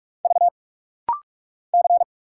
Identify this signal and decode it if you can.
Morse code, heard as a keyed tone, about 44 wpm, 700 Hz. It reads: V C
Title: Single Letter-Number Alt - Rapid-Fire 45wpm